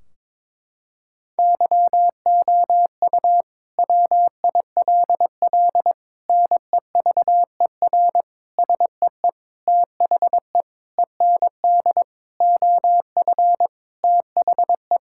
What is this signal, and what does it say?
Morse code, heard as a keyed tone, about 22 wpm, 700 Hz. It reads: YOU WILL NEVER SEE THE END OF THE
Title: Sentences from Top 1600 Words 22wpm